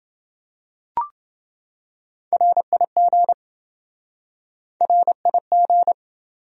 Morse code, heard as a keyed tone, about 27 wpm, 700 Hz. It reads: RIG RIG